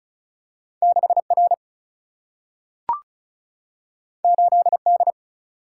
Morse code, heard as a keyed tone, about 36 wpm, 700 Hz. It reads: 6R 8D
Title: Single-Letter Single-Number 35wpm